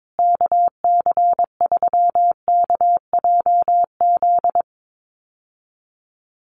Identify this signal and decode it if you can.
Morse code, heard as a keyed tone, about 22 wpm, 700 Hz. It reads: KC3KJZ
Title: USA Call Signs Repeated 2x 22wpm